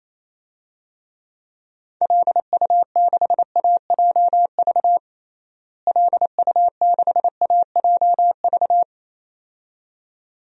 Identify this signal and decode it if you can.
Morse code, heard as a keyed tone, about 28 wpm, 700 Hz. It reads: LU6AJV LU6AJV
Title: DXCC-100 Call Signs - 6 Characters - Repeated 2x - 28wpm